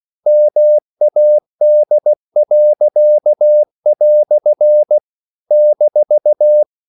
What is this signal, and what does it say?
Morse code, heard as a keyed tone, about 16 wpm, 600 Hz. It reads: MAD." -
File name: CW / Practice / EnglishQuotations0116